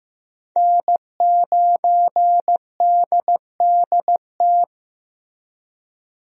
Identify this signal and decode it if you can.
Morse code, heard as a keyed tone, about 15 wpm, 700 Hz. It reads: N9DDT
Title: USA Call Signs - 5 Characters - Repeated 3x - 15wpm